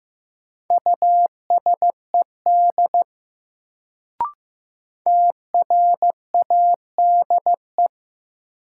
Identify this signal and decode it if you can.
Morse code, heard as a keyed tone, about 15 wpm, 700 Hz. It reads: USED TRADE
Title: Sentences from Top 600 Words - New Word Drill - 15wpm